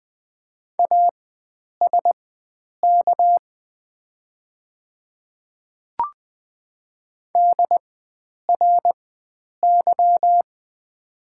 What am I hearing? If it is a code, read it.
Morse code, heard as a keyed tone, about 20 wpm, spaced as for 9 wpm, 700 Hz. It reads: ASK DRY
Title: 4x Character Spacing - Top 500 Words - 20wpm